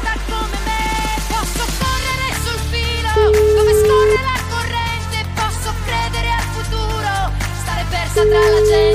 Ringback tone (Europe, 425 Hz), with noise.